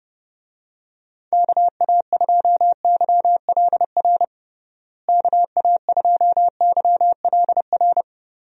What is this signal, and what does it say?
Morse code, heard as a keyed tone, about 30 wpm, 700 Hz. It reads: KA2YLR KA2YLR
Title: USA Call Signs - 6 Characters - Repeated 3x - 30wpm